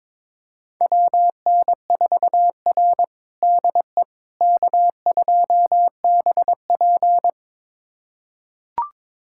Morse code, heard as a keyed tone, about 22 wpm, 700 Hz. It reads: WN4R DE K2BP
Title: USA Callsign DE Callsign - 4 Characters - 22wpm